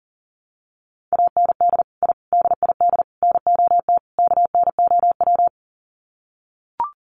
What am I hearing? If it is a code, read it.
Morse code, heard as a keyed tone, about 40 wpm, 700 Hz. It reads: AND I DID NOT KNOW
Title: Sets of 5 Words Encore 40wpm